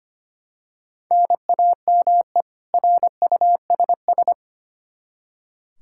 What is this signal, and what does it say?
Morse code, heard as a keyed tone, about 25 wpm, 700 Hz. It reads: NAME RUSS
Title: Practice QSO - Exchange 1 - Beginning - 25wpm